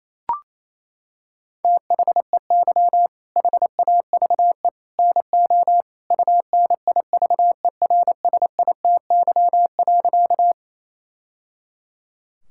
Morse code, heard as a keyed tone, about 28 wpm, 700 Hz. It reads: THEY HAVE NO UNIVERSITY.
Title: Sentences from Top 1000 Words - Review Part 5 - 28wpm